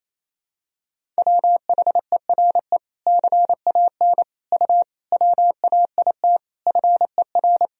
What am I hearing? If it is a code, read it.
Morse code, heard as a keyed tone, about 28 wpm, 700 Hz. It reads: WHERE CAN U WAIT FER
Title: Sentences from Top 500 Words and Abbreviations 28wpm